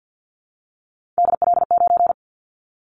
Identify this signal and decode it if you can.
Morse code, heard as a keyed tone, about 50 wpm, 700 Hz. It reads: DL9
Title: Callsign Prefixes for Top 50 DXCC - 50wpm